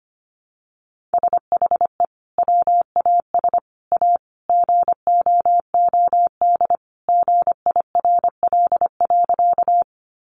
Morse code, heard as a keyed tone, about 25 wpm, 700 Hz. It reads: SHE WAS A GOOD GIRL.
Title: Sentences from Top 400 Words 25wpm